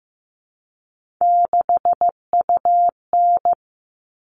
Morse code, heard as a keyed tone, about 15 wpm, 700 Hz. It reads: 6UN
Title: All Cut Numbers - 3-Digit 15wpm